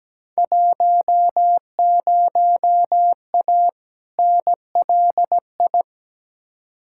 Morse code, heard as a keyed tone, about 17 wpm, 700 Hz. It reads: 10A NLI
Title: Field Day Exchanges Encore - 17wpm